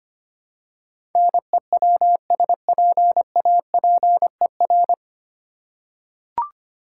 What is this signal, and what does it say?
Morse code, heard as a keyed tone, about 25 wpm, 700 Hz. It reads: NEWSPAPER
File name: practice / 25wpm / 9 Letter Words - First 3 Letters Given - 25wpm